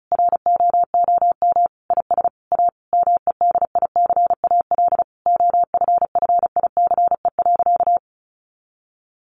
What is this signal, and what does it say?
Morse code, heard as a keyed tone, about 34 wpm, 700 Hz. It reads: ROOM IS A MEDICAL OFFICE.
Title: Additional Easy Phrases 35wpm